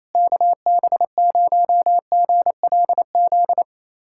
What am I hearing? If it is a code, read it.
Morse code, heard as a keyed tone, about 28 wpm, 700 Hz. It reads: KB0GLZ
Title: USA Callsign DE Callsign - 6 Characters - 28wpm